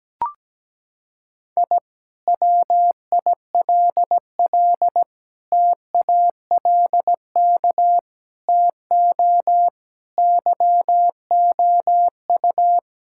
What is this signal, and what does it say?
Morse code, heard as a keyed tone, about 17 wpm, 700 Hz. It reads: I WILL TALK TO YOU
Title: Sentences from Top 1000 Words - Review Part 6 - 17wpm